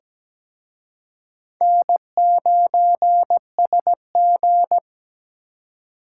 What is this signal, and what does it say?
Morse code, heard as a keyed tone, about 20 wpm, 700 Hz. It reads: N9SG
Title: Speed Racing - USA 1x2 Call Signs 17wpm